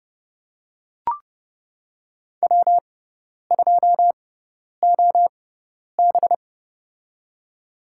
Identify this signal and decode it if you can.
Morse code, heard as a keyed tone, about 30 wpm, spaced as for 10 wpm, 700 Hz. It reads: W2OB
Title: USA Call Signs - 4 Characters - 6x Character Spacing - 30wpm